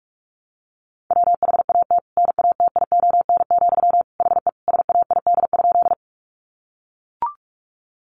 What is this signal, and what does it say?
Morse code, heard as a keyed tone, about 44 wpm, 700 Hz. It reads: WHAT NATION, HE SAID?
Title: Sentences from Top 900 Words 45wpm